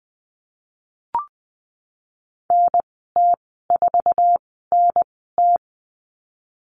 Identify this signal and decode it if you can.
Morse code, heard as a keyed tone, about 20 wpm, spaced as for 15 wpm, 700 Hz. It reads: NT4NT
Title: 062 Lesson 11g - Letter h - Call Signs 20wpm